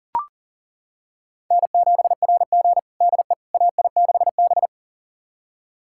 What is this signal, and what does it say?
Morse code, heard as a keyed tone, about 37 wpm, 700 Hz. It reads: N7RG DE AI6B
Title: USA Callsign DE Callsign - 4 Characters - 40wpm